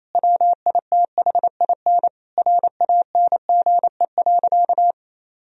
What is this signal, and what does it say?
Morse code, heard as a keyed tone, about 28 wpm, 700 Hz. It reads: WITHIN RANGE.
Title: Sentences from Top 1000 Words - Review Part 4 - 28wpm